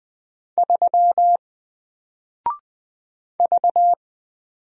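Morse code, heard as a keyed tone, about 20 wpm, 700 Hz. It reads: 3 V
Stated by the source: Single letter and number Morse drill at 20 wpm, from N4LCD.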